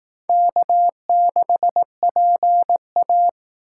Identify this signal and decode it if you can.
Morse code, heard as a keyed tone, about 18 wpm, 700 Hz. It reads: K6PA
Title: Speed Racing - USA Call Signs 15wpm